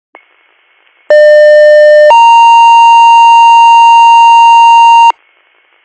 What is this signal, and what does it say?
Two-tone page, heard as a one-second tone, then a three-second tone, about 600 then 930 Hz. It is heard from 1.1 to 5.1 s.